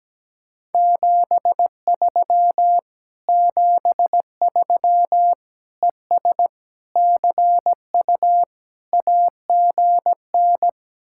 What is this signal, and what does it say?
Morse code, heard as a keyed tone, about 17 wpm, 700 Hz. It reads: 73 73 ES CU AGN
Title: Speed-Racing Common QSO Phrases Encore - 17wpm